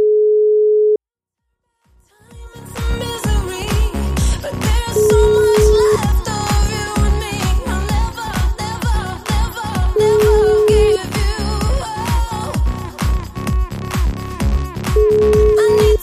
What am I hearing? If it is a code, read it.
Ringback tone (Europe, 425 Hz), with noise.